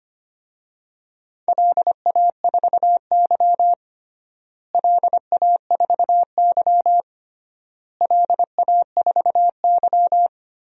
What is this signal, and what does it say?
Morse code, heard as a keyed tone, about 25 wpm, 700 Hz. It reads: LA4Y LA4Y LA4Y
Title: DXCC-100 Call Signs - 4 Characters - Repeated 3x - 25wpm